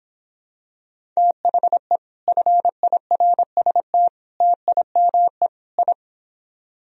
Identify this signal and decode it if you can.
Morse code, heard as a keyed tone, about 26 wpm, 700 Hz. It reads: THE FIRST TIME I